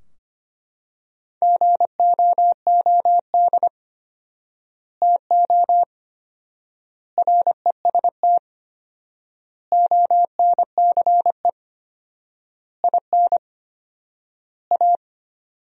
Morse code, heard as a keyed tone, about 25 wpm, 700 Hz. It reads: GOOD TO REST ONCE IN A
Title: 4x Word-Spacing - Sentences from Top 500 Words - 25wpm